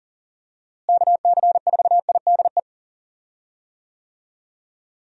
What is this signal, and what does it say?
Morse code, heard as a keyed tone, about 40 wpm, 700 Hz. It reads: KC4IDE
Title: USA Call Signs - 6 Characters - Repeated 2x - 40wpm